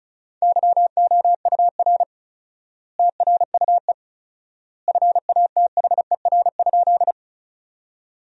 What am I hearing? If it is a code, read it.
Morse code, heard as a keyed tone, about 36 wpm, 700 Hz. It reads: YOUR TRUE FATHER?
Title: 4x Word-Spacing - Sentences from Top 400 Words - 35wpm